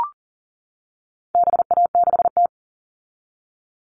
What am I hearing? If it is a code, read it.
Morse code, heard as a keyed tone, about 40 wpm, 700 Hz. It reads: BA6T
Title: All Cut Numbers - 4-Digit 40wpm